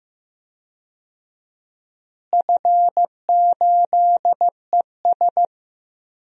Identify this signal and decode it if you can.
Morse code, heard as a keyed tone, about 15 wpm, 700 Hz. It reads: F8ES
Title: DXCC-100 Call Signs - 4 Characters - Repeated 2x - 15wpm